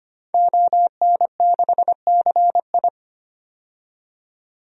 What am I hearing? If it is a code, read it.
Morse code, heard as a keyed tone, about 25 wpm, 700 Hz. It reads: ON6CI
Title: DXCC-100 Call Signs - 5 Characters - 25wpm